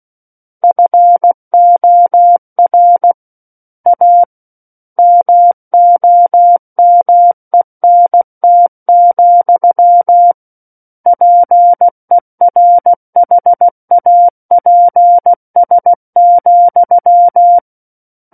Morse code, heard as a keyed tone, about 16 wpm, 700 Hz. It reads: FOR A MOMENT, PERHAPS,
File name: Book One / 07-Chapter 7 - 16 WPM